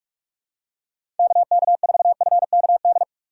Morse code, heard as a keyed tone, about 44 wpm, 700 Hz. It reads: KK4FKD
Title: USA Call Signs - 4x Character Spacing - 45wpm